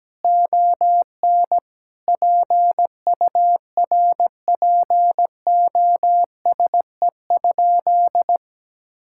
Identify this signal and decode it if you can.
Morse code, heard as a keyed tone, about 17 wpm, 700 Hz. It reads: ON PURPOSE?
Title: Sentences from Top 1000 Words - Review Part 5 - 17wpm